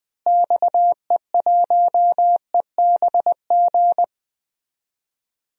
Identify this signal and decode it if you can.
Morse code, heard as a keyed tone, about 20 wpm, 700 Hz. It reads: XE1EBG
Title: DXCC-100 Call Signs - 6 Characters - Repeated 2x - 20wpm